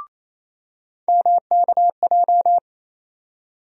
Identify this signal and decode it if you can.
Morse code, heard as a keyed tone, about 28 wpm, 700 Hz. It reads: MKJ
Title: USA Call Sign Postfixes - 3 Letters 28wpm